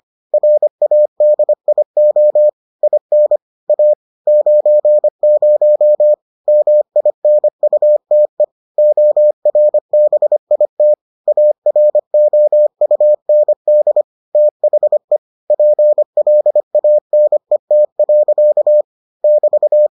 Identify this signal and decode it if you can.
Morse code, heard as a keyed tone, about 25 wpm, 600 Hz. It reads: RADIO IN A 90 MINUTE ORBIT AROUND THE PLANET. <BT>